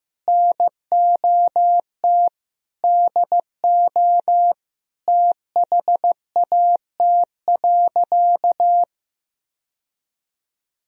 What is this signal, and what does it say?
Morse code, heard as a keyed tone, about 15 wpm, 700 Hz. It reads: NOT DO THAT.
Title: Sentences from Top 1000 Words - Review Part 4 - 15wpm